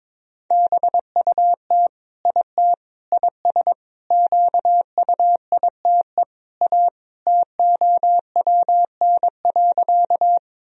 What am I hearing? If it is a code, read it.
Morse code, heard as a keyed tone, about 22 wpm, 700 Hz. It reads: BUT IT IS QUITE A TOWN.
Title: Sentences from Top 1000 Words - Review Part 2 - 22wpm